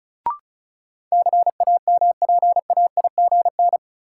Morse code, heard as a keyed tone, about 34 wpm, 700 Hz. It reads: CAMPAIGN
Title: Sentences from Top 1200 Words - New Word Drill Alt - 35wpm